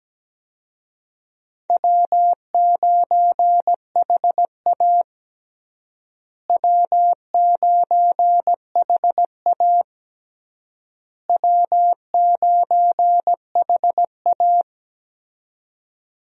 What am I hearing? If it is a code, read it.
Morse code, heard as a keyed tone, about 17 wpm, 700 Hz. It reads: W9HA W9HA W9HA
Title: USA Call Signs - 4 Characters - Repeated 3x - 17wpm